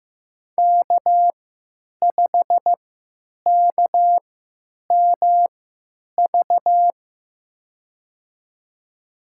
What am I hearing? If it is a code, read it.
Morse code, heard as a keyed tone, about 15 wpm, spaced as for 9 wpm, 700 Hz. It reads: K5KMV
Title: USA Call Signs - 3x Character Spacing - 15wpm